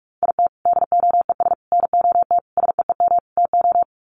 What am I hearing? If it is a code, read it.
Morse code, heard as a keyed tone, about 44 wpm, 700 Hz. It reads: IT DOES NOT SEEM TO
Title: Sets of 5 Words Encore2 45wpm